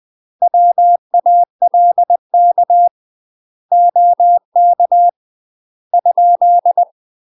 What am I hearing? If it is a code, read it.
Morse code, heard as a keyed tone, about 20 wpm, 700 Hz. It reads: WALK OK ?
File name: UK MASTER QSOs TRAINING / 20WPM / UK MASTER QSOs 0051